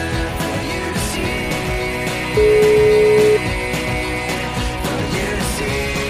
Ringback tone (Europe, 425 Hz), with noise.